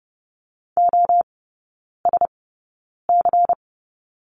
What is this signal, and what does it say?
Morse code, heard as a keyed tone, about 30 wpm, spaced as for 9 wpm, 690 Hz. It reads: OSC
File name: Lesson 16e - Letter c - Three Characters 30wpm